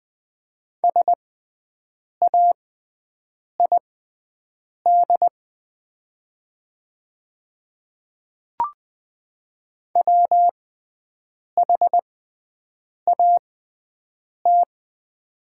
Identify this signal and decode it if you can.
Morse code, heard as a keyed tone, about 20 wpm, spaced as for 7 wpm, 700 Hz. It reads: SAID WHAT
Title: 6x Character Spacing - Top 100 Words - 20wpm